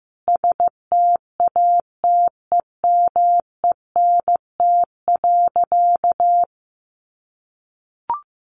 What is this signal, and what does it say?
Morse code, heard as a keyed tone, about 15 wpm, 700 Hz. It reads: STATEMENT.
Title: Sentences from Top 1200 Words 15wpm